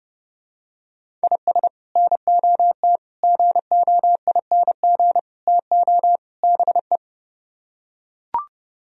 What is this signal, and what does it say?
Morse code, heard as a keyed tone, about 31 wpm, 700 Hz. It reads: IS NOT GOING TO BE